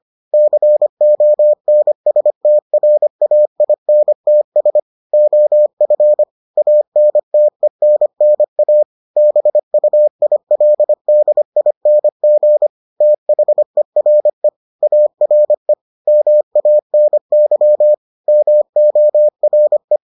Morse code, heard as a keyed tone, about 25 wpm, 600 Hz. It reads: CONSTRAINTS OF ANTENNA BUILDING THERE ARE MANY MORE